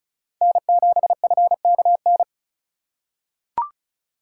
Morse code, heard as a keyed tone, about 36 wpm, 700 Hz. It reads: N7FKN
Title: USA Call Signs - 5 Characters - 4x Character Spacing - 35wpm